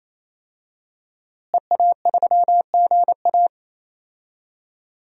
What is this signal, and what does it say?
Morse code, heard as a keyed tone, about 28 wpm, 700 Hz. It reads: EA3GA
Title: DXCC-100 Call Signs - 5 Characters - Repeated 3x - 28wpm